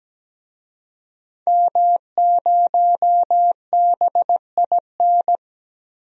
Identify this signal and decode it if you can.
Morse code, heard as a keyed tone, about 17 wpm, 700 Hz. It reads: M0BIN
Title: DXCC-100 Call Signs - 5 Characters - 17wpm